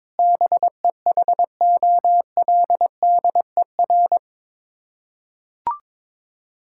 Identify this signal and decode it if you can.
Morse code, heard as a keyed tone, about 22 wpm, 700 Hz. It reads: BEHOLDER
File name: practice / 22wpm / Sets of 7 Words - 22wpm